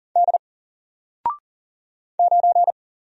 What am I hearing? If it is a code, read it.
Morse code, heard as a keyed tone, about 39 wpm, 700 Hz. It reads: D 9
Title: Single Letter-Number Alt 40wpm